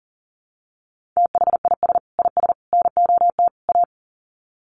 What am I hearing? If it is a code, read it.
Morse code, heard as a keyed tone, about 40 wpm, 700 Hz. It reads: THIS IS NOT A
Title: Sets of 4 Words 40wpm